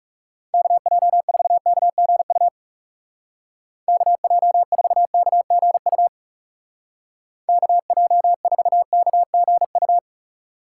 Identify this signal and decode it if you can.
Morse code, heard as a keyed tone, about 36 wpm, 700 Hz. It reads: KJ4KGU KJ4KGU KJ4KGU
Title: Speed Racing - USA Call Signs 30wpm